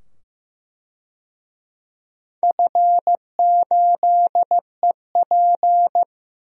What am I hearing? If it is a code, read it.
Morse code, heard as a keyed tone, about 15 wpm, 700 Hz. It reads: F8EP
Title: DXCC-100 Call Signs - 4 Characters - 15wpm